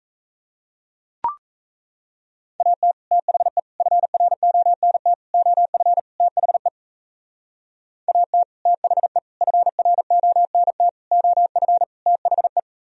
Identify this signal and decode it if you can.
Morse code, heard as a keyed tone, about 34 wpm, 700 Hz. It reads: AT THE FRONT OF THE AT THE FRONT OF THE
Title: Speed-Racing - Sets of 5 Words 28wpm